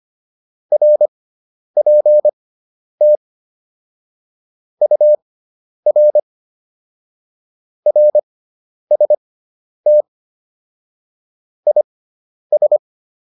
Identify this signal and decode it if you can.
Morse code, heard as a keyed tone, about 25 wpm, spaced as for 10 wpm, 600 Hz. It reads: RPT UR RST IS